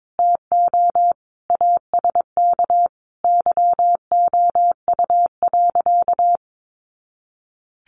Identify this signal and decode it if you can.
Morse code, heard as a keyed tone, about 22 wpm, 700 Hz. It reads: TO ASK YOU.
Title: Sentences from Top 400 Words 22wpm